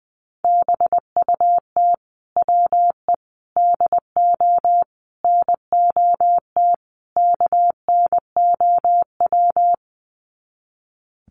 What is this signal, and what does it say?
Morse code, heard as a keyed tone, about 20 wpm, 700 Hz. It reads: BUT WE DO NOT KNOW
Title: Sets of 5 Words Encore2 20wpm